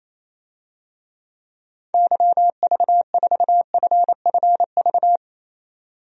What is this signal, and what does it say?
Morse code, heard as a keyed tone, about 28 wpm, 700 Hz. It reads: YV4FFV
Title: DXCC-100 Call Signs - 6 Characters - 28wpm